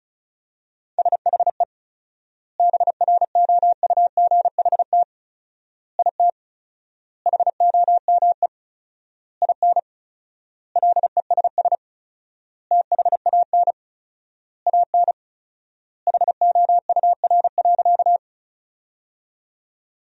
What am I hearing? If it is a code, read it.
Morse code, heard as a keyed tone, about 34 wpm, 700 Hz. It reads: SHE BROUGHT IT HOME IN LESS THAN AN HOUR.
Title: 4x Word-Spacing - Sentences from Top 500 Words - 35wpm